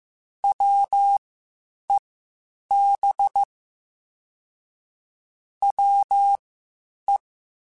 Morse code, heard as a keyed tone, about 15 wpm, spaced as for 8 wpm, 800 Hz. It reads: WEB WE
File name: Quotes Spaced / Quotes Spaced-mp3s / Deceive sp @13